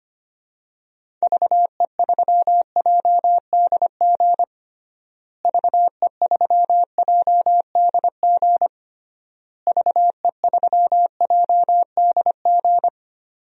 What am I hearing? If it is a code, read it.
Morse code, heard as a keyed tone, about 25 wpm, 700 Hz. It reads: VE3JDG VE3JDG VE3JDG
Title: DXCC-100 Call Signs - 6 Characters - Repeated 3x - 25wpm